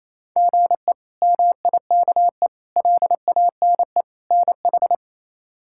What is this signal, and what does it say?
Morse code, heard as a keyed tone, about 28 wpm, 700 Hz. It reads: GE MIKE LANE NH
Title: K1USN SST QSO Practice - 28wpm